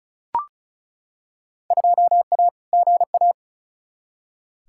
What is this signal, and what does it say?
Morse code, heard as a keyed tone, about 36 wpm, 700 Hz. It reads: 2A GA
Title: Field Day Exchanges Encore - 35wpm